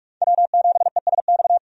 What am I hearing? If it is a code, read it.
Morse code, heard as a keyed tone, about 46 wpm, 700 Hz. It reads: W7ESX
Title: USA Call Signs - 5 Characters - 45wpm